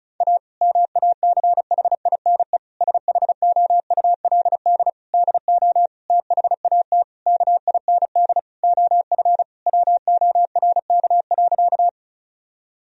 Morse code, heard as a keyed tone, about 34 wpm, 700 Hz. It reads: A MACHINE SHOULD DO THAT KIND OF WORK.
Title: Sentences from Top 1000 Words - Review Part 2 - 35wpm